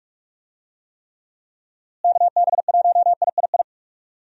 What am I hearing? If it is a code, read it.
Morse code, heard as a keyed tone, about 44 wpm, 700 Hz. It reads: KB1III
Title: USA Call Signs - 6 Characters - Repeated 3x - 45wpm